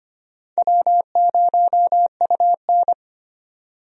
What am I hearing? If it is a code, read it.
Morse code, heard as a keyed tone, about 25 wpm, 700 Hz. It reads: W0UN
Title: USA Call Signs - 4 Characters - 25wpm